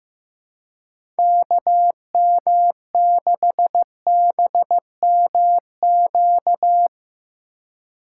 Morse code, heard as a keyed tone, about 15 wpm, 700 Hz. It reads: KM6BMQ
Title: USA Call Signs - 6 Characters - 4x Character Spacing - 15wpm